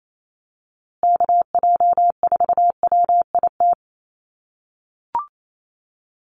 Morse code, heard as a keyed tone, about 28 wpm, 700 Hz. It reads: KJ4WIT
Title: USA Call Signs Repeated 3x 28wpm